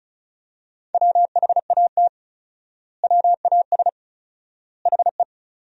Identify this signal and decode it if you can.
Morse code, heard as a keyed tone, about 36 wpm, 700 Hz. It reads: WHAT WAS HE
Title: 4x Word-Spacing - Sentences from Top 100 Words - 35wpm